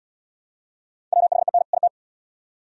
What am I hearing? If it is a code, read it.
Morse code, heard as a keyed tone, about 72 wpm, 700 Hz. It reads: WHEN IT